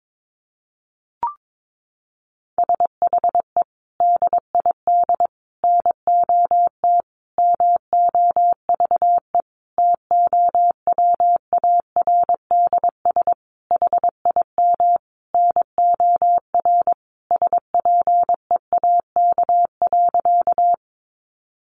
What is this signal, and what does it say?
Morse code, heard as a keyed tone, about 22 wpm, 700 Hz. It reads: SHE DID NOT MOVE TOWARDS HIM NOR SPEAK.
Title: Sentences from Top 1500 Words 22wpm